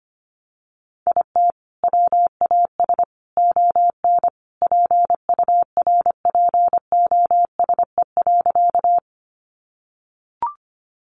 Morse code, heard as a keyed tone, about 25 wpm, 700 Hz. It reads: IT WAS ON PURPOSE.
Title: Sentences from Top 800 Words 25wpm